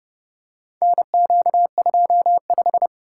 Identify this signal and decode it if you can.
Morse code, heard as a keyed tone, about 30 wpm, 700 Hz. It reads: NQ25